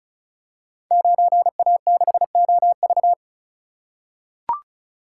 Morse code, heard as a keyed tone, about 36 wpm, 700 Hz. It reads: 9A6OV
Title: Code Groups - 5 Letter:Numbers - 35wpm